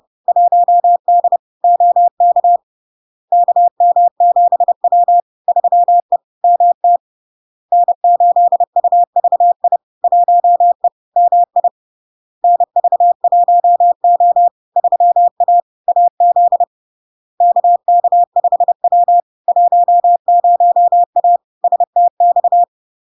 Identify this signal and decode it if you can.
Morse code, heard as a keyed tone, about 30 wpm, 700 Hz. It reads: 1D OK KM7W 3E MT N8UVI 1E MI NV1O 3A AZ KK5W 10A STX
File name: LARK-FD-2022-Log-in-CW-30wpm-700-Hz